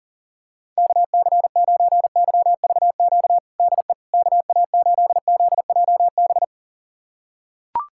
Morse code, heard as a keyed tone, about 40 wpm, 700 Hz. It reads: KC9YVQ DE KA8ZJB
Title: USA Callsign DE Callsign - 6 Characters - 40wpm